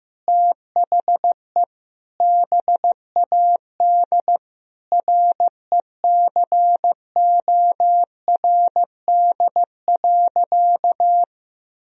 Morse code, heard as a keyed tone, about 15 wpm, 700 Hz. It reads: THE BAD RECORD.
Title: Sentences from Top 1000 Words - Review Part 3 - 15wpm